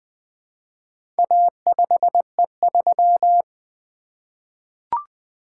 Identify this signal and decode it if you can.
Morse code, heard as a keyed tone, about 20 wpm, 700 Hz. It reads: A5E3
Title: Four Letter-Number Combinations - 20wpm